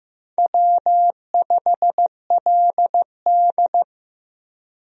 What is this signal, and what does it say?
Morse code, heard as a keyed tone, about 16 wpm, 700 Hz. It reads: W5LD
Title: USA Call Signs - 4 Characters - 15wpm